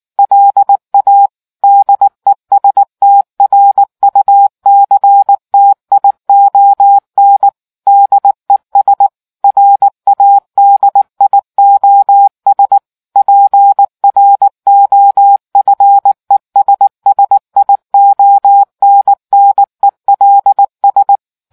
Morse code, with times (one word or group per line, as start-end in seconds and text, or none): 0.2-1.3 LA
1.6-7.5 DESTRUCTION
7.9-9.1 DES
9.4-12.8 RADIOS
13.2-21.2 PROFESSIONNELS